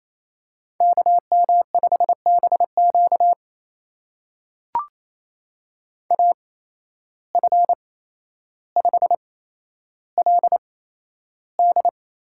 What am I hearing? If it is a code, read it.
Morse code, heard as a keyed tone, about 28 wpm, spaced as for 12 wpm, 700 Hz. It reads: KM5BQ AF5LD
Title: USA Call Signs - 5 Characters - 8x Character Spacing - 28wpm